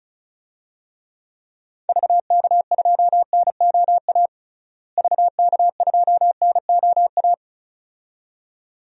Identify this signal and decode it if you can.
Morse code, heard as a keyed tone, about 36 wpm, 700 Hz. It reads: VK2NOA VK2NOA
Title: DXCC-100 Call Signs - 6 Characters - Repeated 2x - 35wpm